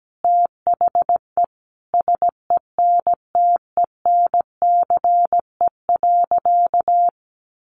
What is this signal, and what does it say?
Morse code, heard as a keyed tone, about 17 wpm, 700 Hz. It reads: THE SENTENCE.
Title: Additional Easy Phrases 17wpm